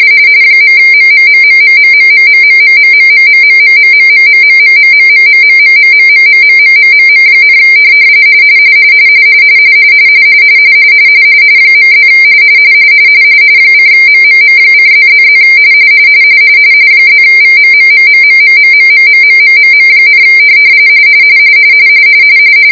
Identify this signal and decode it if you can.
RTTY, about 45.45 baud. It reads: .MMMMMMMMMMMMMMMMMMMMMMMMMMMMMMMMMMMMMM, .+ :MMM:+++ +VMMMMMMMMMMMMMM,